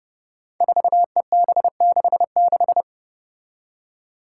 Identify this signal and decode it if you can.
Morse code, heard as a keyed tone, about 31 wpm, 700 Hz. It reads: 4EB66
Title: Five Challenging Letter-Number Combinations 30wpm